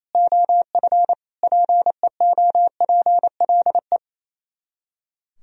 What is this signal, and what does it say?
Morse code, heard as a keyed tone, about 26 wpm, 700 Hz. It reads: OF PEOPLE
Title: Speed-Racing - Sets of 5 Words 28wpm